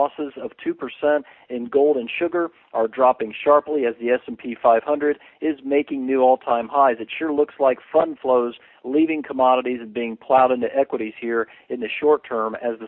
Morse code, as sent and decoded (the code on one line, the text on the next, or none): none